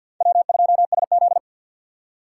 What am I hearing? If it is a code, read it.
Morse code, heard as a keyed tone, about 62 wpm, 700 Hz. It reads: W2SZ
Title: Speed Racing - USA 1x2 Call Signs 50wpm